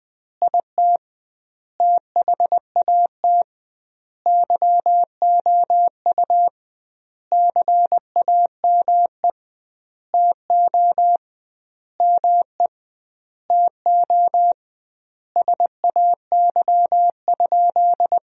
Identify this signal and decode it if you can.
Morse code, heard as a keyed tone, about 20 wpm, 700 Hz. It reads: IT THAT YOU CAME TO ME TO SAY?
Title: 2x Word-Spacing - Sentences from Top 200 Words - 20wpm